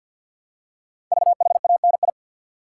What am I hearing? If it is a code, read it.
Morse code, heard as a keyed tone, about 50 wpm, 690 Hz. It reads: VHANI
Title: Code Groups - 5 Letters - 50wpm